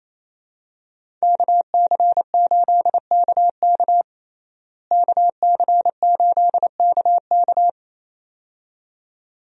Morse code, heard as a keyed tone, about 28 wpm, 700 Hz. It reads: KC8KK KC8KK
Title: USA Call Signs - 5 Characters - Repeated 2x - 28wpm